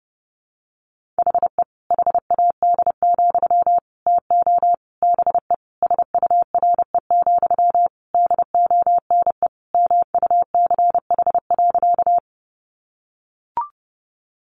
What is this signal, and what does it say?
Morse code, heard as a keyed tone, about 30 wpm, 700 Hz. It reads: HE HAD, TO BE SURE, DONE MUCH.
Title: Sentences from Top 500 Words 30wpm